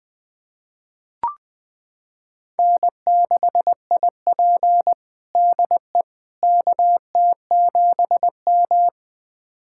Morse code, heard as a keyed tone, about 20 wpm, 700 Hz. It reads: N6IP DE KT7M
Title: USA Callsign DE Callsign - 4 Characters - 20wpm